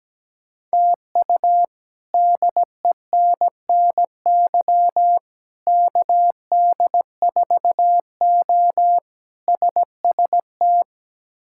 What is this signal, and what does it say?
Morse code, heard as a keyed tone, about 17 wpm, 700 Hz. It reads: TU DENNY KD4O SST
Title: K1USN SST QSO Practice - 17wpm